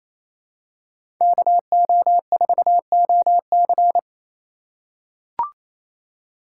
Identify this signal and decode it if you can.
Morse code, heard as a keyed tone, about 28 wpm, 700 Hz. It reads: KO4OC
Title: USA Call Signs - 5 Characters - 4x Character Spacing - 28wpm